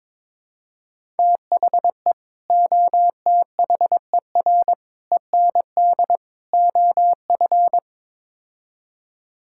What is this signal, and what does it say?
Morse code, heard as a keyed tone, about 22 wpm, 700 Hz. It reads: THE OTHER END OF
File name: practice / 22wpm / Build A Phrase - 5-Word Phrases - 22wpm